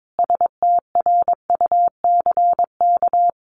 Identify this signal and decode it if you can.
Morse code, heard as a keyed tone, about 22 wpm, 700 Hz. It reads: STRUCK
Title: A Sound Of Thunder 22wpm